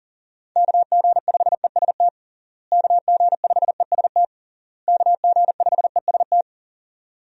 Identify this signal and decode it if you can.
Morse code, heard as a keyed tone, about 40 wpm, 700 Hz. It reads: KG5EST KG5EST KG5EST
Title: USA Call Signs - 6 Characters - Repeated 3x - 40wpm